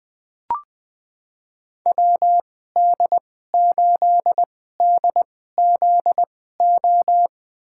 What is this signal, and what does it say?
Morse code, heard as a keyed tone, about 20 wpm, spaced as for 14 wpm, 700 Hz. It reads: WD8DZO
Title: USA Call Signs - 2x Character Spacing - 20wpm